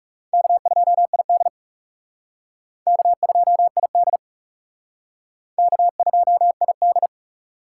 Morse code, heard as a keyed tone, about 38 wpm, 700 Hz. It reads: K2ID K2ID K2ID